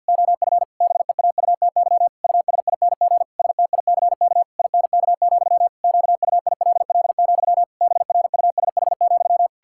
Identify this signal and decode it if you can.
Morse code, heard as a keyed tone, about 52 wpm, 700 Hz. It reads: OF BEAUTY USING STICK INK, QUILL, BRUSH,
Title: Pangram Challenge v2 - 50wpm